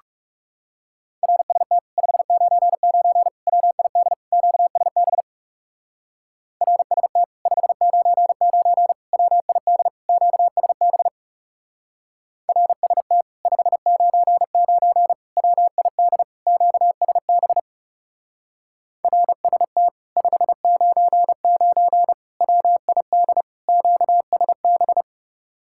Morse code, heard as a keyed tone, about 35 wpm, 700 Hz. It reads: RST 599 WID QSB RST 599 WID QSB RST 599 WID QSB RST 599 WID QSB
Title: Speed-Racing Common QSO Phrases - 30wpm